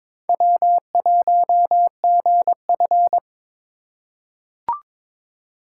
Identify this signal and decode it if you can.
Morse code, heard as a keyed tone, about 22 wpm, 700 Hz. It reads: W1GF
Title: USA Call Signs - 4 Characters - 2x Character Spacing - 22wpm